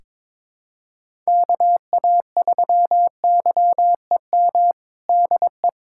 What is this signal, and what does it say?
Morse code, heard as a keyed tone, about 22 wpm, 700 Hz. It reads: KA3YEM DE
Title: USA Callsign DE Callsign 22wpm